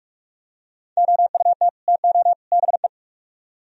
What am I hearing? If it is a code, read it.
Morse code, heard as a keyed tone, about 38 wpm, 700 Hz. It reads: OUT TO BE